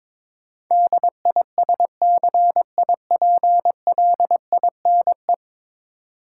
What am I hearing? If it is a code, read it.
Morse code, heard as a keyed tone, about 22 wpm, 700 Hz. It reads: DISCIPLINE
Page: morse-code-ninja-practice-22wpm